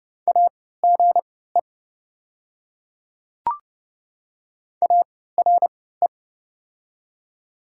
Morse code, heard as a keyed tone, about 30 wpm, spaced as for 17 wpm, 700 Hz. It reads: AGE ARE
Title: 3x Character Spacing - Top 500 Words - 30wpm